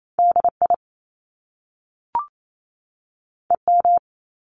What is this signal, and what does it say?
Morse code, heard as a keyed tone, about 28 wpm, 700 Hz. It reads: DI EM
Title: Top Two Letter Combinations in Words 28wpm